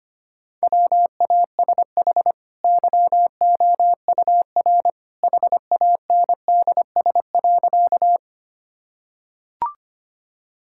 Morse code, heard as a keyed tone, about 25 wpm, 700 Hz. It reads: WASH YOUR HANDS.
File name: Easy Sentences with Hints - 25wpm